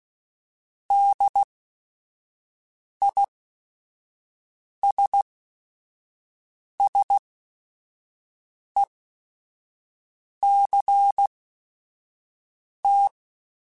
Morse code, heard as a keyed tone, about 16 wpm, spaced as for 5 wpm, 800 Hz. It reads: DISSECT
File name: General / General-mp3s / ISS @5